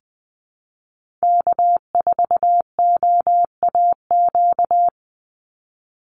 Morse code, heard as a keyed tone, about 20 wpm, 700 Hz. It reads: K4OAQ
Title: Call Signs Repeated 3x - 20wpm